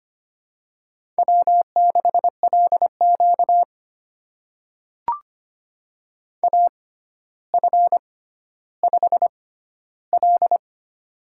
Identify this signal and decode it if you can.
Morse code, heard as a keyed tone, about 25 wpm, spaced as for 13 wpm, 700 Hz. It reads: W6LQ AF5L
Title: USA Call Signs - 4 Characters - 6x Character Spacing - 25wpm